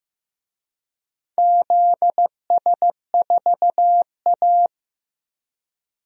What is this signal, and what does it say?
Morse code, heard as a keyed tone, about 15 wpm, 700 Hz. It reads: ZS4A
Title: DXCC-100 Call Signs - 4 Characters - 15wpm